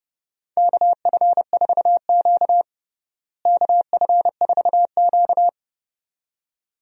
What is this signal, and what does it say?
Morse code, heard as a keyed tone, about 30 wpm, 700 Hz. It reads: KF4Q KF4Q
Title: USA Call Signs - 4 Characters - Repeated 2x - 30wpm